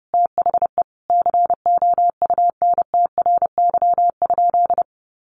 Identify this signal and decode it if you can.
Morse code, heard as a keyed tone, about 28 wpm, 700 Hz. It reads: THE COUNTRY?
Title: Sentences from Top 1400 Words 30wpm